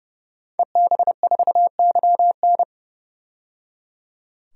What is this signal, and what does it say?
Morse code, heard as a keyed tone, about 30 wpm, 700 Hz. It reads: EB4YN
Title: DXCC-100 Call Signs - 5 Characters - 30wpm